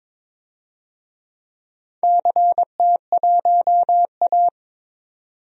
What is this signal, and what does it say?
Morse code, heard as a keyed tone, about 22 wpm, 700 Hz. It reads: CT1A
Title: DXCC-100 Call Signs - 4 Characters - Repeated 3x - 22wpm